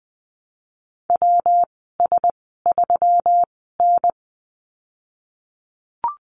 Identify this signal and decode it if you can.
Morse code, heard as a keyed tone, about 20 wpm, spaced as for 15 wpm, 700 Hz. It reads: WS3N
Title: 128 Lesson 21g - Number 6 - Call Signs 20wpm